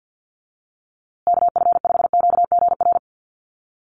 Morse code, heard as a keyed tone, about 50 wpm, 700 Hz. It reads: KF5QGR
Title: USA Call Signs Repeated 3x 50wpm